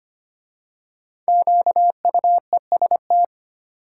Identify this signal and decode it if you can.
Morse code, heard as a keyed tone, about 25 wpm, 700 Hz. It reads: QUEST
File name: 5 Letter Words - First 3 Letters Given - 25wpm